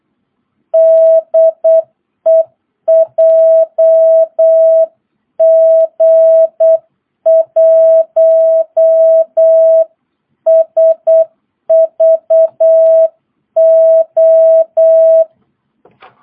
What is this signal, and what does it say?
Morse code, heard as a keyed tone, about 8 wpm, 660 Hz. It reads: DE JG1SVO
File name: jh8cbh / 2022cw